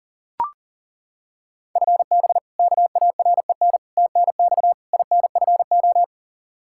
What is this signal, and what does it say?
Morse code, heard as a keyed tone, about 40 wpm, 700 Hz. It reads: FB KAREN TNX INFO
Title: K1USN SST QSO Practice 40wpm